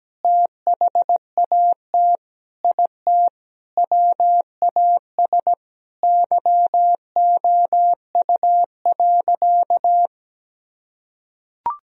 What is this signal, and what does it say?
Morse code, heard as a keyed tone, about 17 wpm, 700 Hz. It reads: THAT IT WAS YOU.
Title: Sentences from Top 1000 Words - Review Part 4 - 17wpm